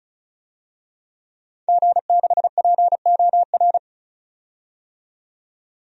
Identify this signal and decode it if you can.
Morse code, heard as a keyed tone, about 36 wpm, 700 Hz. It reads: G6POR
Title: DXCC-100 Call Signs - 5 Characters - 35wpm